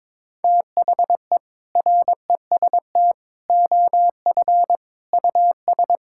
Morse code, heard as a keyed tone, about 22 wpm, 700 Hz. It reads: THE REST OF US
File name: practice / 22wpm / Sentences with 4-Word Phrases - 22wpm